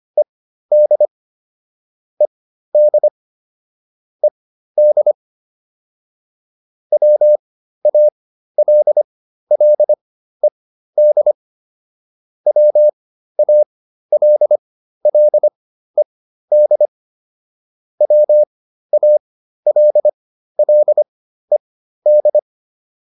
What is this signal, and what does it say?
Morse code, heard as a keyed tone, about 25 wpm, spaced as for 13 wpm, 600 Hz. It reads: ED ED ED WALLED WALLED WALLED